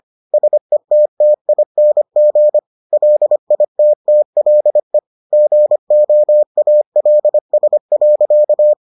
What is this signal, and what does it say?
Morse code, heard as a keyed tone, about 25 wpm, 600 Hz. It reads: SETTING LITTLE GOALS.